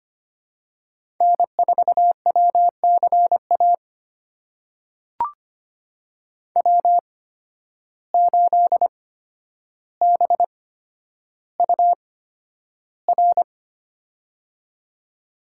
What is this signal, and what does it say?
Morse code, heard as a keyed tone, about 25 wpm, spaced as for 11 wpm, 700 Hz. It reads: N4WCA W8BUR
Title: USA Call Signs - 5 Characters - 8x Character Spacing - 25wpm